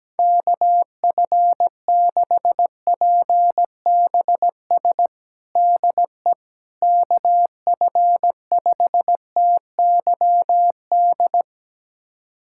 Morse code, heard as a keyed tone, about 17 wpm, 700 Hz. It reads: KF6PBS DE KF5TYD
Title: USA Callsign DE Callsign - 6 Characters - 17wpm